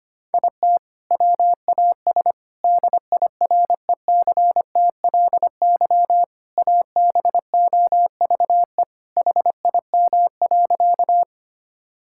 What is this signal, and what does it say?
Morse code, heard as a keyed tone, about 25 wpm, 700 Hz. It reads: IT WAS DIRECTLY ABOVE HIM.
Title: Sentences from Top 1000 Words - Review Part 5 - 25wpm